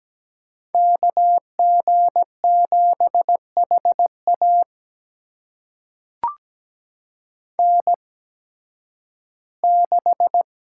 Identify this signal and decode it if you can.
Morse code, heard as a keyed tone, about 17 wpm, spaced as for 11 wpm, 700 Hz. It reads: KG7HA N6
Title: USA Call Signs - 5 Characters - 8x Character Spacing - 17wpm